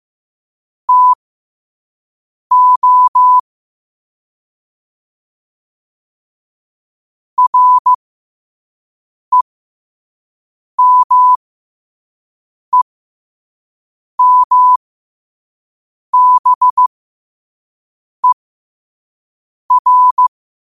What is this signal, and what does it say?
Morse code, heard as a keyed tone, about 15 wpm, spaced as for 5 wpm, 1000 Hz. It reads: TO REMEMBER
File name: qotdfiles / 5wpm